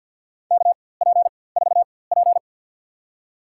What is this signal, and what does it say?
Morse code, heard as a keyed tone, about 53 wpm, spaced as for 24 wpm, 700 Hz. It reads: KP4P
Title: USA Call Signs - 4x Character Spacing - 50wpm